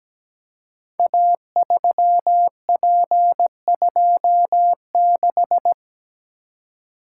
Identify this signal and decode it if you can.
Morse code, heard as a keyed tone, about 17 wpm, 700 Hz. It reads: A3P26
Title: Code Groups - 5 Letter:Numbers - 17wpm